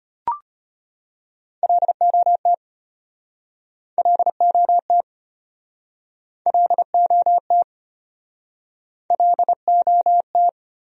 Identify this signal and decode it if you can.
Morse code, heard as a keyed tone, about 37 wpm, 700 Hz. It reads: LOT LOT LOT LOT